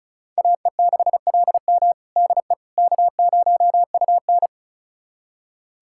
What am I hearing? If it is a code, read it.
Morse code, heard as a keyed tone, about 36 wpm, 700 Hz. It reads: AE6LM DE K0UN
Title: USA Callsign DE Callsign 35wpm